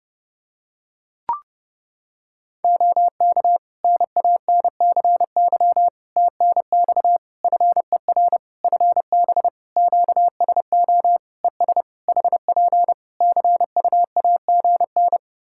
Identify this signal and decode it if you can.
Morse code, heard as a keyed tone, about 30 wpm, 700 Hz. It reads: OK NANCY TNX FER FB QSO ES HP CUAGN
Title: Practice QSO - Exchange 4 - Ending - 30wpm